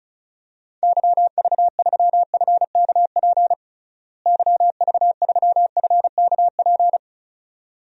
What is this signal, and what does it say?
Morse code, heard as a keyed tone, about 34 wpm, 700 Hz. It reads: YV3FKP YV3FKP
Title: DXCC-100 Call Signs - 6 Characters - Repeated 2x - 35wpm